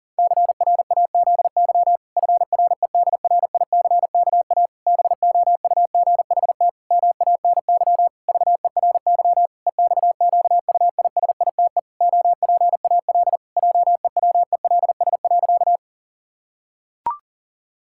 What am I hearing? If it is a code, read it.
Morse code, heard as a keyed tone, about 40 wpm, 700 Hz. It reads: CRAZY FREDRICKA BOUGHT MANY VERY EXQUISITE OPAL JEWELS.
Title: Pangram Challenge v2 - 40wpm